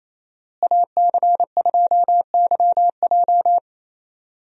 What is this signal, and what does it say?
Morse code, heard as a keyed tone, about 28 wpm, 700 Hz. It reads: AC2YJ
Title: USA Call Signs - 5 Characters - Repeated 2x - 28wpm